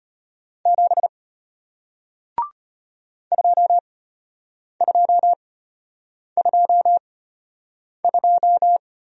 Morse code, heard as a keyed tone, about 33 wpm, spaced as for 8 wpm, 700 Hz. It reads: 7 2222